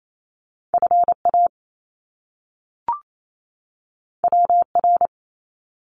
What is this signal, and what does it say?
Morse code, heard as a keyed tone, about 28 wpm, 700 Hz. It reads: FA WR
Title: Top Two Letter Combinations in Words 28wpm